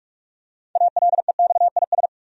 Morse code, heard as a keyed tone, about 46 wpm, 700 Hz. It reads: ALEXIS
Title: USA Common Names - 45wpm